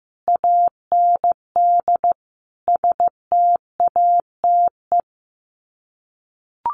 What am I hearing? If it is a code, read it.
Morse code, heard as a keyed tone, about 15 wpm, 700 Hz. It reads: AND STATE
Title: Sets of 5 Words 15wpm